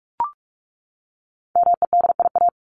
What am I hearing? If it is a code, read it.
Morse code, heard as a keyed tone, about 44 wpm, 700 Hz. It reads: MEDIA
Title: Top 501-750 Words 45wpm